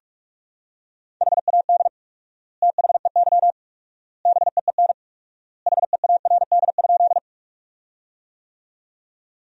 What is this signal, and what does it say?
Morse code, heard as a keyed tone, about 46 wpm, 700 Hz. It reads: HAD THEY BEEN HEARD?
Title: 4x Word-Spacing - Sentences from Top 400 Words - 45wpm